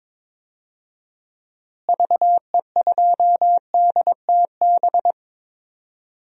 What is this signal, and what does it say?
Morse code, heard as a keyed tone, about 22 wpm, 700 Hz. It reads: VE2DTB
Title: DXCC-100 Call Signs - 6 Characters - Repeated 3x - 22wpm